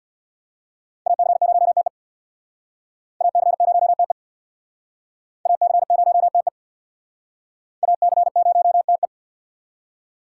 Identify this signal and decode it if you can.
Morse code, heard as a keyed tone, about 64 wpm, 700 Hz. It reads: AC0TE AC0TE AC0TE AC0TE